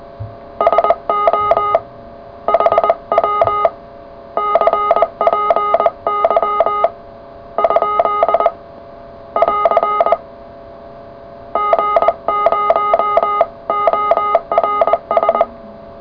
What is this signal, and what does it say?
Morse code, heard as a keyed tone, about 20 wpm, 1240 Hz, with noise. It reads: SO HW CPY ? <AR> G0ORS
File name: full / cw qso